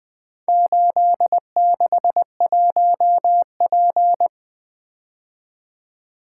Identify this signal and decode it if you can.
Morse code, heard as a keyed tone, about 20 wpm, 700 Hz. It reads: 861P
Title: Four Letter-Number Combinations - 20wpm